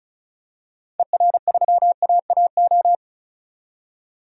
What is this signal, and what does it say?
Morse code, heard as a keyed tone, about 36 wpm, 700 Hz. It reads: ER3AAO
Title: DXCC-100 Call Signs - 6 Characters - 35wpm